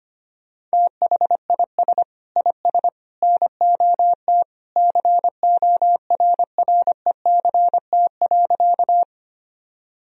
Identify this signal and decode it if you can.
Morse code, heard as a keyed tone, about 25 wpm, 700 Hz. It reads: THIS IS NOT CORRECT.
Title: Sentences from Top 1000 Words - Review Part 2 - 25wpm